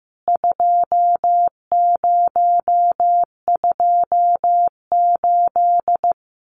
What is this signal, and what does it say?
Morse code, heard as a keyed tone, about 15 wpm, 700 Hz. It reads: 2028
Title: CWT Exchange 15wpm